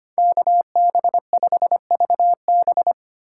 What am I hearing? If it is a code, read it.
Morse code, heard as a keyed tone, about 32 wpm, 700 Hz. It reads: KB5VB